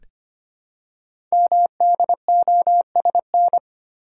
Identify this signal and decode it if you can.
Morse code, heard as a keyed tone, about 25 wpm, 700 Hz. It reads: MDOSN
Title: Code Groups - 5 Letter:Numbers - 25wpm